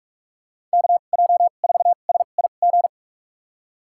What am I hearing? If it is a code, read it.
Morse code, heard as a keyed tone, about 46 wpm, spaced as for 33 wpm, 690 Hz. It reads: KJ4SIG
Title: USA Call Signs - 6 Characters - 2x Character Spacing - 45wpm